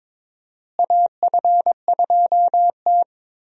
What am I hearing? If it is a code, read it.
Morse code, heard as a keyed tone, about 23 wpm, 700 Hz. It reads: AF2T
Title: USA Call Signs - 4 Characters - Repeated 3x - 22wpm